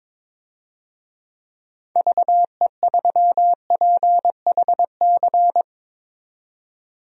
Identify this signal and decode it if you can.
Morse code, heard as a keyed tone, about 22 wpm, 700 Hz. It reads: VE3PHC
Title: DXCC-100 Call Signs - 6 Characters - 22wpm